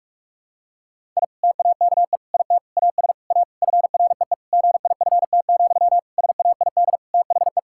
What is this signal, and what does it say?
Morse code, heard as a keyed tone, about 44 wpm, 700 Hz. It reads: I TAKE IT AS A FREE GIFT, SAID THE
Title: Sentences from Top 1000 Words - Review Part 6 - 45wpm